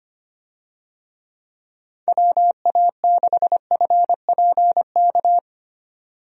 Morse code, heard as a keyed tone, about 25 wpm, 700 Hz. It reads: WA6FPK
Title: USA Call Signs - 6 Characters - 25wpm